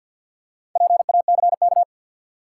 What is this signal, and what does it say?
Morse code, heard as a keyed tone, about 53 wpm, 700 Hz. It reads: PACK
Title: Four Letter Words - 50wpm